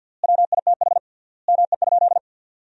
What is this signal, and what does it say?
Morse code, heard as a keyed tone, about 50 wpm, 700 Hz. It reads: WITH ME?
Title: 3x Word-Spacing - Sentences from Top 300 Words - 50wpm